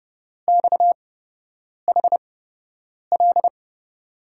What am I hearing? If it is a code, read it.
Morse code, heard as a keyed tone, about 30 wpm, spaced as for 8 wpm, 700 Hz. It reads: XHL
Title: USA Call Signs - 8x Character Spacing - 30wpm